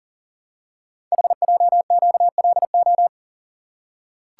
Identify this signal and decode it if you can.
Morse code, heard as a keyed tone, about 40 wpm, 700 Hz. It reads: HJQLO
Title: Code Groups - 5 Letters - 40wpm